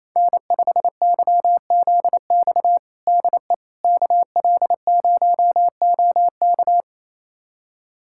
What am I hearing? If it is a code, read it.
Morse code, heard as a keyed tone, about 28 wpm, 700 Hz. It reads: N5YZX DE KL0OK
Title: USA Callsign DE Callsign - 5 Characters - 28wpm